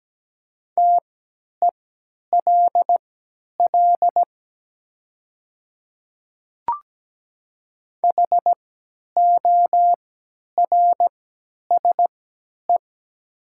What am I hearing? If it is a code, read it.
Morse code, heard as a keyed tone, about 17 wpm, spaced as for 10 wpm, 700 Hz. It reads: TELL HORSE
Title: 3x Character Spacing - Top 300 Words - 17wpm